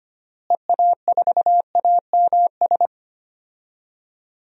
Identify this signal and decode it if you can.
Morse code, heard as a keyed tone, about 25 wpm, 700 Hz. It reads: EA4AMS
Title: DXCC-100 Call Signs - 6 Characters - Repeated 3x - 25wpm